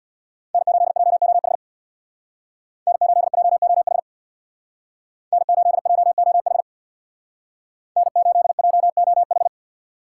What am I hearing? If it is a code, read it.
Morse code, heard as a keyed tone, about 64 wpm, 700 Hz. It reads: N8JOH N8JOH N8JOH N8JOH